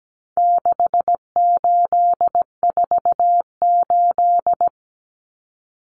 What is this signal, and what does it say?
Morse code, heard as a keyed tone, about 17 wpm, 700 Hz. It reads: 6848
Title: Numbers - 4-Digit 17wpm